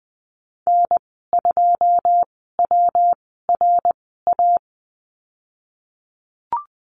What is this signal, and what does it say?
Morse code, heard as a keyed tone, about 20 wpm, spaced as for 15 wpm, 700 Hz. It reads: N2WRA
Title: 122 Lesson 20g - Number 3 - Call Signs 20wpm